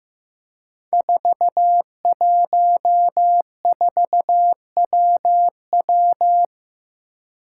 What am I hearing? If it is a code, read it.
Morse code, heard as a keyed tone, about 15 wpm, 700 Hz. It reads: 414WW
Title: Five Challenging Letter-Number Combinations 15wpm